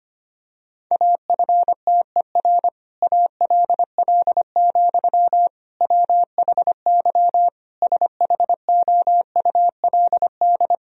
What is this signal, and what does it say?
Morse code, heard as a keyed tone, about 25 wpm, 700 Hz. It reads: AFTER ALL, WHY SHOULD
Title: Sentences from Top 1000 Words - Review Part 2 - 25wpm